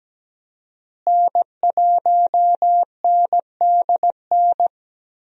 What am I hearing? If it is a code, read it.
Morse code, heard as a keyed tone, about 17 wpm, 700 Hz. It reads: N1NDN
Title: USA Call Signs - 5 Characters - Repeated 2x - 17wpm